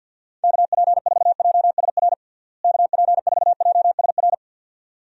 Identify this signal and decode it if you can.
Morse code, heard as a keyed tone, about 49 wpm, 700 Hz. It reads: KP4JSR KP4JSR
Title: USA Call Signs - 6 Characters - Repeated 2x - 50wpm